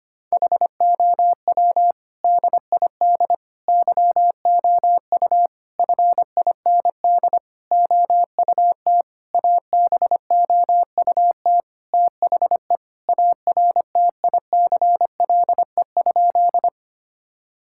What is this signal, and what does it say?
Morse code, heard as a keyed tone, about 25 wpm, 700 Hz. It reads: HOW DID YOU FIND OUT ABOUT THE ARTICLE?
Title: Sentences from Top 1000 Words - Review Part 5 - 25wpm